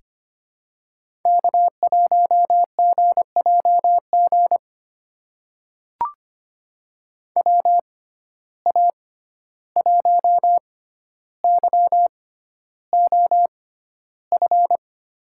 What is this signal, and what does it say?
Morse code, heard as a keyed tone, about 25 wpm, spaced as for 12 wpm, 700 Hz. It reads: K1GJG WA1YOF
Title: USA Call Signs - 6x Character Spacing - 25wpm